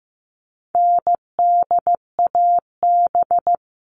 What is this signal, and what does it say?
Morse code, heard as a keyed tone, about 15 wpm, 700 Hz. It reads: NDAB
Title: All Cut Numbers - 4-Digit 15wpm